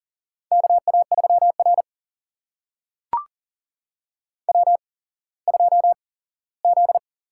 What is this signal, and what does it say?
Morse code, heard as a keyed tone, about 38 wpm, spaced as for 19 wpm, 700 Hz. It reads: KA3R W2Z
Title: USA Call Signs - 4 Characters - 8x Character Spacing - 40wpm